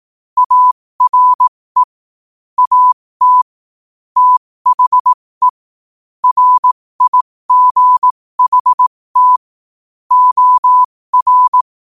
Morse code, heard as a keyed tone, about 18 wpm, spaced as for 15 wpm, 1000 Hz. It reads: ARE AT THE RIGHT OR
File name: 15wpm